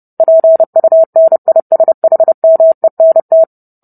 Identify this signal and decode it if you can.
Morse code, heard as a keyed tone, about 30 wpm, 650 Hz. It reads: PUNISHMENT